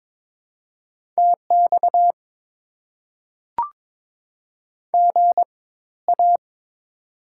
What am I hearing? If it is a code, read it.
Morse code, heard as a keyed tone, about 22 wpm, spaced as for 14 wpm, 690 Hz. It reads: TX GA
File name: Top 90 QSO Elements - 4x Character Spacing - 22wpm